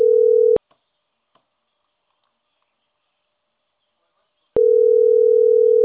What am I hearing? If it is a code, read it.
Ringback tone (North America), 440 + 480 Hz.